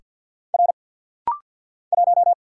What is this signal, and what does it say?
Morse code, heard as a keyed tone, about 50 wpm, 700 Hz. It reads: R 1
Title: Single Letter-Number Alt - Rapid-Fire 50wpm